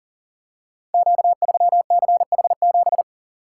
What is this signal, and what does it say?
Morse code, heard as a keyed tone, about 42 wpm, 690 Hz. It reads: Q3CH7
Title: Code Groups - 5 Letter:Numbers - 40wpm